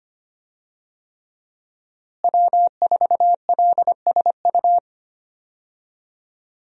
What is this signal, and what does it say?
Morse code, heard as a keyed tone, about 25 wpm, 700 Hz. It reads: W4LSU
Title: USA Call Signs - 5 Characters - Repeated 3x - 25wpm